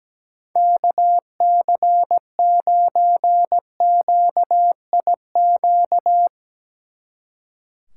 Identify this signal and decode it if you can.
Morse code, heard as a keyed tone, about 19 wpm, 700 Hz. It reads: KC9QIQ
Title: Speed Racing - USA 2x3 Call Signs 17wpm